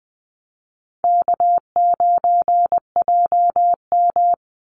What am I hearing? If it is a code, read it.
Morse code, heard as a keyed tone, about 20 wpm, 700 Hz. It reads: K9JM
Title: Easy Call Signs Repeated 3x - 20wpm